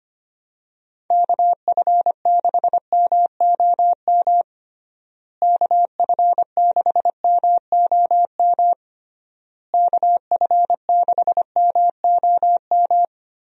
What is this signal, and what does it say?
Morse code, heard as a keyed tone, about 25 wpm, 700 Hz. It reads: KF6MOM KF6MOM KF6MOM
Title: USA Call Signs - 6 Characters - Repeated 3x - 25wpm